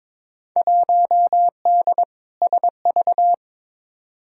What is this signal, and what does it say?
Morse code, heard as a keyed tone, about 22 wpm, 700 Hz. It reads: 1D SV
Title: Field Day Exchanges - 22wpm